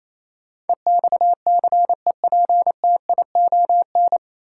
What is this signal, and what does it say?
Morse code, heard as a keyed tone, about 28 wpm, 700 Hz. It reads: EXCEPTION
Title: Sets of 4 Words - New Word Drill - 28wpm